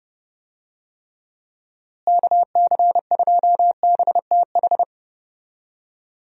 Morse code, heard as a keyed tone, about 31 wpm, 700 Hz. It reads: KC2BTH
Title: USA Call Signs - 6 Characters - Repeated 2x - 30wpm